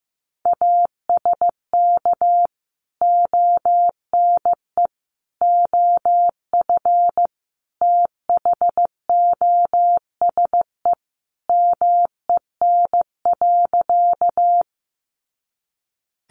Morse code, heard as a keyed tone, about 15 wpm, 700 Hz. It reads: ASK ONE OF THOSE MEN.
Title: Sentences from Top 300 Words 15wpm